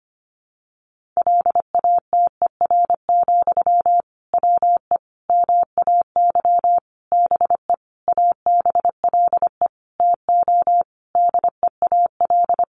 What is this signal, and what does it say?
Morse code, heard as a keyed tone, about 25 wpm, 700 Hz. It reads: LATER, WE MAY BE ABLE TO DEAL
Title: Sentences from Top 800 Words 25wpm